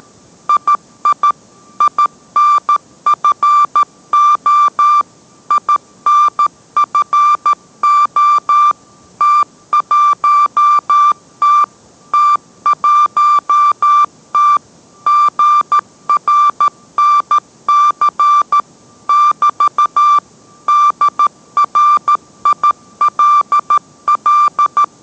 Morse code, heard as a keyed tone, about 13 wpm, 1180 Hz. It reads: II INFO INFO T1T T1T GRNC <BT> DRILL